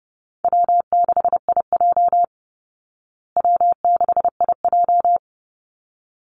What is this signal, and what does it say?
Morse code, heard as a keyed tone, about 30 wpm, 700 Hz. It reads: W6IJ W6IJ
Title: Easy Call Signs Repeated 2x - 30wpm